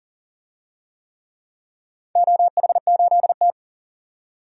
Morse code, heard as a keyed tone, about 38 wpm, 700 Hz. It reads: OH8T
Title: DXCC-100 Call Signs - 4 Characters - 40wpm